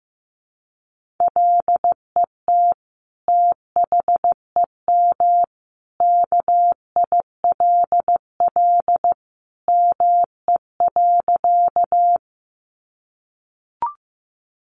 Morse code, heard as a keyed tone, about 15 wpm, 700 Hz. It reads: LET THEM KILL ME.
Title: Sentences from Top 1400 Words 15wpm